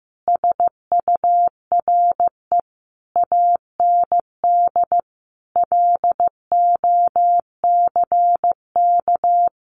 Morse code, heard as a keyed tone, about 15 wpm, 700 Hz. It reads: SURE AND LOCK
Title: Sentences from Top 1300 Words 15wpm